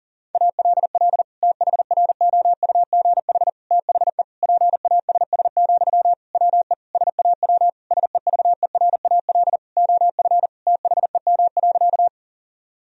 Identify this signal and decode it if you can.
Morse code, heard as a keyed tone, about 40 wpm, 700 Hz. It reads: ALL THROUGH THE PASS, WE SAW SEVERAL OF THEM.
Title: Sentences from Top 1000 Words - Review Part 2 - 40wpm